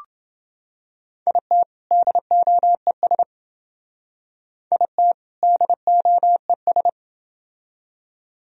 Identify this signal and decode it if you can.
Morse code, heard as a keyed tone, about 28 wpm, 700 Hz. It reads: IT DOES IT DOES